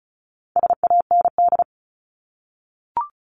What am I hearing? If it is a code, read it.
Morse code, heard as a keyed tone, about 34 wpm, 700 Hz. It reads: SAND
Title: Top 1501-1750 Words 35wpm